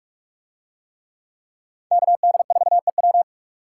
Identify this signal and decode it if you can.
Morse code, heard as a keyed tone, about 39 wpm, 700 Hz. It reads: KD4EW